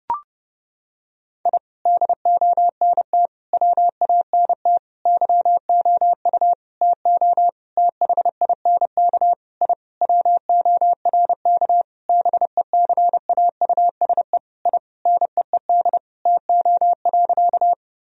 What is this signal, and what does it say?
Morse code, heard as a keyed tone, about 32 wpm, 700 Hz. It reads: I DONT WANT YOU TO THINK I WORK BECAUSE I NEED TO.
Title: Sentences from Top 1000 Words - Review Part 3 - 30wpm